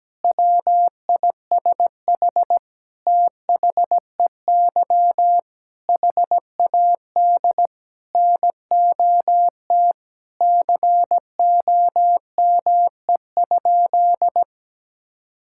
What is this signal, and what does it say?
Morse code, heard as a keyed tone, about 17 wpm, 700 Hz. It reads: WISH THEY HAD NOT COME?
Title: Sentences from Top 1600 Words 17wpm